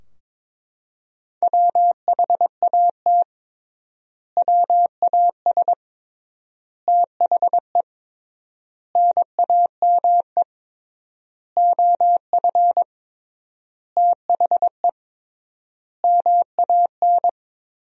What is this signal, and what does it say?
Morse code, heard as a keyed tone, about 22 wpm, 700 Hz. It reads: WHAT WAS THE NAME OF THE MAN
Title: 3x Word-Spacing - Sentences from Top 300 Words - 22wpm